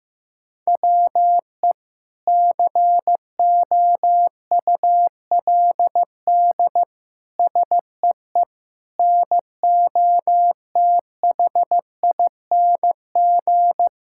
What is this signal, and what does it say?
Morse code, heard as a keyed tone, about 16 wpm, 700 Hz. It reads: WE COULD SEE NOTHING
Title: Sentences from Top 1000 Words - Review Part 5 - 15wpm